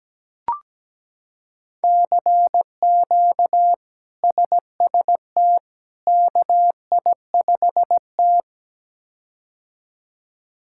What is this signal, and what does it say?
Morse code, heard as a keyed tone, about 17 wpm, 700 Hz. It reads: CQ SST KI5T
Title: K1USN SST QSO Practice - 17wpm